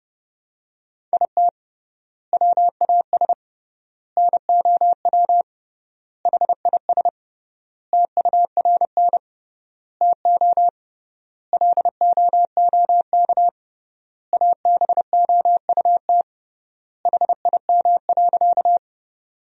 Morse code, heard as a keyed tone, about 30 wpm, 700 Hz. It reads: IT WAS NOW HIS TURN TO LOOK ABOUT HIM.
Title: 3x Word-Spacing - Sentences from Top 200 Words - 30wpm